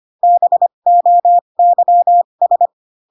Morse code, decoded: BOYS

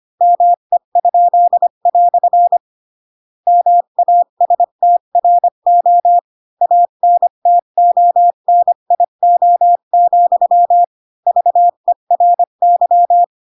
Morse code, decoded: ME?" MASTRO ANTONIO, VERY

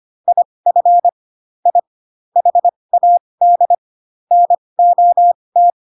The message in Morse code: IF I HAD NOT

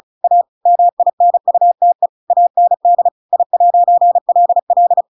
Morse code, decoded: A MINUTE AND I'LL

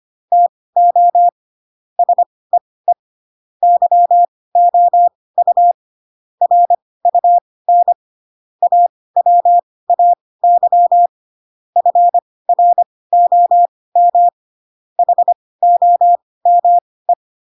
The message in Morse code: TO SEE YOU RUN AWAY FROM HOME

